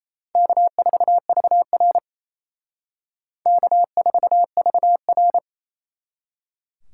Morse code, heard as a keyed tone, about 33 wpm, 700 Hz. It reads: K4VR K4VR
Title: Speed Racing - USA 1x2 Call Signs 28wpm